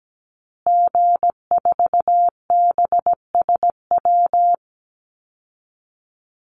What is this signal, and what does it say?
Morse code, heard as a keyed tone, about 17 wpm, 700 Hz. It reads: G4BSW
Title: Call Signs Repeated 2x - 17wpm